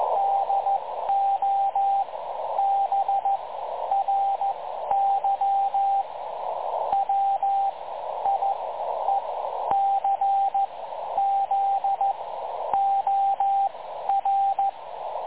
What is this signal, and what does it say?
Morse code, heard as a keyed tone, about 14 wpm, spaced as for 10 wpm, 780 Hz, with radio noise. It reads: DOBRY WIECZOR
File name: dlazaawansowanych / nagraniazpasma / websdr recording 3565.0kHz